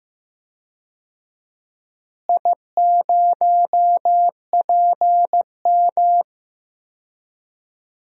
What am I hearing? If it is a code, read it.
Morse code, heard as a keyed tone, about 15 wpm, 700 Hz. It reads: I0PM
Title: DXCC-100 Call Signs - 4 Characters - Repeated 2x - 15wpm